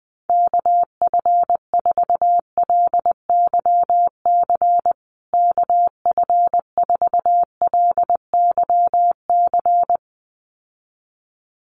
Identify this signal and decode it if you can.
Morse code, heard as a keyed tone, about 20 wpm, 700 Hz. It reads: KF4LYC KF4LYC
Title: USA Call Signs Repeated 2x 20wpm